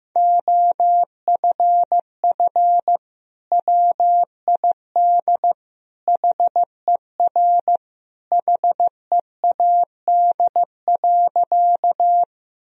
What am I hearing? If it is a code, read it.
Morse code, heard as a keyed tone, about 15 wpm, 700 Hz. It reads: OFF WID HER HEAD.
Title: Sentences from Top 300 Words and Abbreviations 15wpm